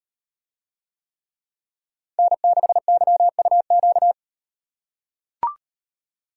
Morse code, heard as a keyed tone, about 37 wpm, 700 Hz. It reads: N6YUQ